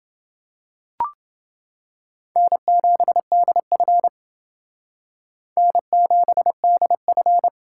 Morse code, heard as a keyed tone, about 28 wpm, 700 Hz. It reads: N7DF N7DF